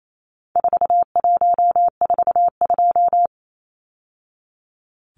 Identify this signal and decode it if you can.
Morse code, heard as a keyed tone, about 28 wpm, 700 Hz. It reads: 4142
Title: Numbers - 4-Digit 28wpm